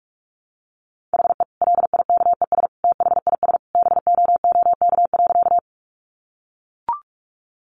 Morse code, heard as a keyed tone, about 44 wpm, 700 Hz. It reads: HE LIKES THIS BOOK.
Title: Additional Easy Phrases 45wpm